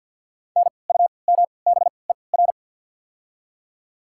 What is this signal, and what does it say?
Morse code, heard as a keyed tone, about 54 wpm, 700 Hz. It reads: N U M B E R
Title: 3x Character Spacing - Top 300 Words - 50wpm